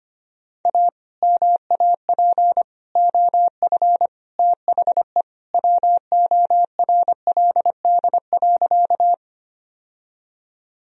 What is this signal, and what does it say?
Morse code, heard as a keyed tone, about 25 wpm, 700 Hz. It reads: A MAP OF THE WORLD.
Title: Sentences from Top 1000 Words - Review Part 2 - 25wpm